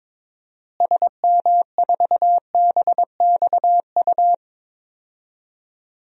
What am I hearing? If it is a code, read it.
Morse code, heard as a keyed tone, about 21 wpm, 700 Hz. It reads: SM4BXU
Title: DXCC-100 Call Signs - 6 Characters - Repeated 3x - 22wpm